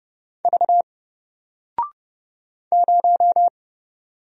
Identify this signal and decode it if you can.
Morse code, heard as a keyed tone, about 29 wpm, 700 Hz. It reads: V 0
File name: Single-Letter-Number-30wpm